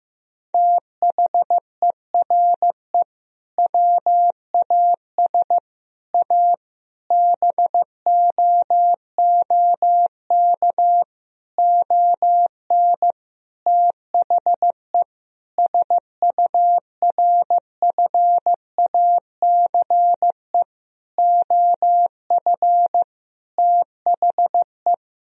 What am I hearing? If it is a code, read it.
Morse code, heard as a keyed tone, about 15 wpm, 700 Hz. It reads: THERE WAS A BOOK ON THE SURFACE OF THE
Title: Sentences from Top 1000 Words - Review Part 3 - 15wpm